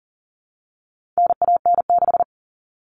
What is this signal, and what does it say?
Morse code, heard as a keyed tone, about 40 wpm, 700 Hz. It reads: NAN6
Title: All Cut Numbers - 4-Digit 40wpm